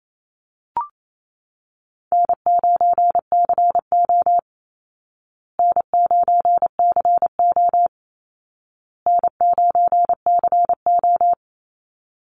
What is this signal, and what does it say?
Morse code, heard as a keyed tone, about 28 wpm, 700 Hz. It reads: N9CO N9CO N9CO
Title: Call Signs Repeated 2x - 28wpm